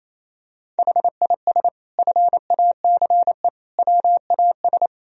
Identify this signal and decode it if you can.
Morse code, heard as a keyed tone, about 28 wpm, 700 Hz. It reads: HIS FACE WAS
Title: Sentences from Top 1000 Words - Review Part 4 - 28wpm